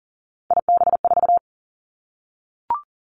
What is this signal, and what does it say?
Morse code, heard as a keyed tone, about 42 wpm, 690 Hz. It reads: IB4
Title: Three Challenging Letter-Number Combinations 40wpm